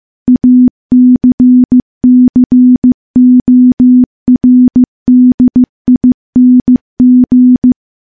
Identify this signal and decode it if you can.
Morse code, heard as a keyed tone, about 15 wpm, 250 Hz. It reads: ACCORDING
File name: morse / morse-training-15WPM